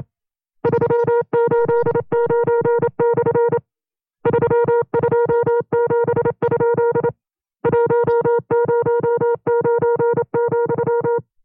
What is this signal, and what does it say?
Morse code, heard as a keyed tone, about 27 wpm, 470 Hz. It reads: 389/ 327? 109,
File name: 25WPM